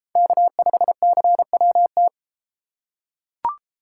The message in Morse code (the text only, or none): K5CWT